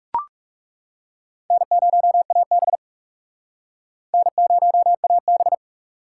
Morse code, heard as a keyed tone, about 35 wpm, 700 Hz. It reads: N0AB N0AB
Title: Speed Racing - USA 1x2 Call Signs 30wpm